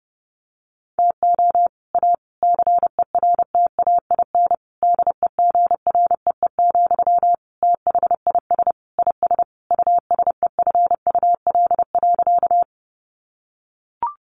Morse code, heard as a keyed tone, about 30 wpm, 700 Hz. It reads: TO A CERTAIN DEGREE, THIS IS USEFUL.
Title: Sentences from Top 1500 Words 30wpm